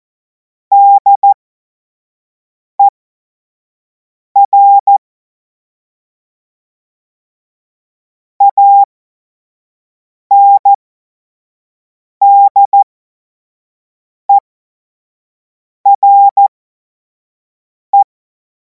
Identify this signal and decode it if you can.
Morse code, heard as a keyed tone, about 14 wpm, spaced as for 5 wpm, 800 Hz. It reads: DER ANDERE
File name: Morsekurs / Witze / Witz 03. Februar 2026 25 F70